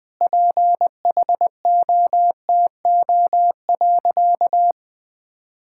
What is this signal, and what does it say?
Morse code, heard as a keyed tone, about 20 wpm, 700 Hz. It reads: PHOTO.
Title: Sentences with 4-Word Phrases - 20wpm